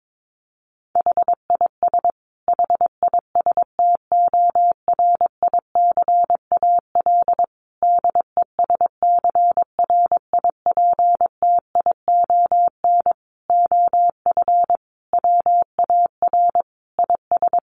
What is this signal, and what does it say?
Morse code, heard as a keyed tone, about 22 wpm, 700 Hz. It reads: HIS HISTORICAL DESCRIPTION OF WAR IS
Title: Sentences from Top 1300 Words 22wpm